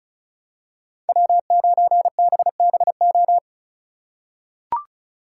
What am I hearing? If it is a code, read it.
Morse code, heard as a keyed tone, about 36 wpm, 700 Hz. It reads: W9BBO
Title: USA Call Signs - 3x Character Spacing - 35wpm